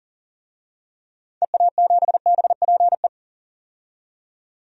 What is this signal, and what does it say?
Morse code, heard as a keyed tone, about 40 wpm, 700 Hz. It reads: EA7BPE
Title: DXCC-100 Call Signs - 6 Characters - 40wpm